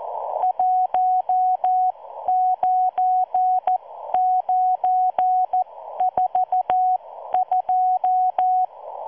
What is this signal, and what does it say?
Morse code, heard as a keyed tone, about 14 wpm, spaced as for 12 wpm, 740 Hz. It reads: 19942